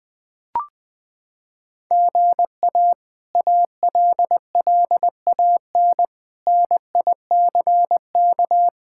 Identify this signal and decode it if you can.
Morse code, heard as a keyed tone, about 20 wpm, 700 Hz. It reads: GA ALLAN NICK
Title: K1USN SST QSO Practice - 20wpm